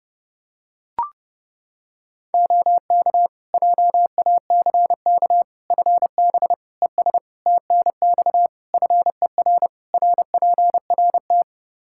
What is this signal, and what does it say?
Morse code, heard as a keyed tone, about 30 wpm, 700 Hz. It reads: OK JACK FB ES TNX FER RPRT
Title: Practice QSO - Exchange 2 - Info - 30wpm